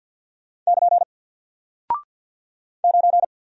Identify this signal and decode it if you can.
Morse code, heard as a keyed tone, about 48 wpm, 700 Hz. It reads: <KN> 9
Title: Single Character (Letter:Number:Punctuation:Prosign) Alt 50wpm